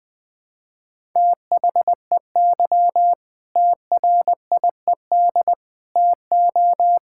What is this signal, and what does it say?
Morse code, heard as a keyed tone, about 20 wpm, 700 Hz. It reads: THEY TRIED TO
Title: Sentences with 8-Word Phrases - Spoken Beginning - 20wpm